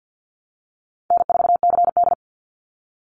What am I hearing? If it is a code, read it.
Morse code, heard as a keyed tone, about 50 wpm, 700 Hz. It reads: N4CD
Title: Call Signs Repeated 2x - 50wpm